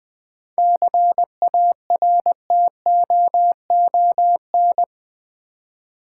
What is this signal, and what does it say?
Morse code, heard as a keyed tone, about 20 wpm, 700 Hz. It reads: CARTOON
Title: 7 Letter Words - First 3 Letters Given - 20wpm